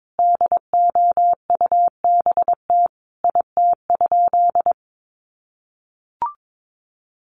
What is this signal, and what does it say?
Morse code, heard as a keyed tone, about 22 wpm, 700 Hz. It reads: DOUBT IT?
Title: Sentences from Top 1500 Words 22wpm